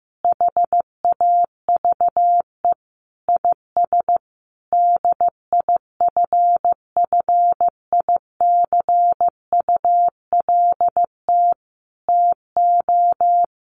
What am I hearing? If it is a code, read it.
Morse code, heard as a keyed tone, about 15 wpm, 700 Hz. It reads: HAVE IS DIFFICULT TO
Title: Sentences from Top 1500 Words 15wpm